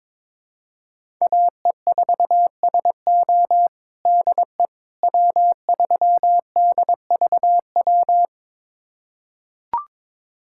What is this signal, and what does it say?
Morse code, heard as a keyed tone, about 21 wpm, 700 Hz. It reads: AE4SO DE W3DVW
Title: USA Callsign DE Callsign - 5 Characters - 22wpm